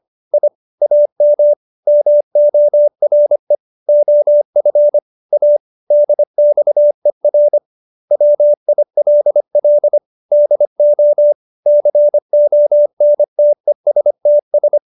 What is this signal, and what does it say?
Morse code, heard as a keyed tone, about 25 wpm, 590 Hz. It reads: I AM MORE OF A DXER WILL DO CONTESTS
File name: QSO315 25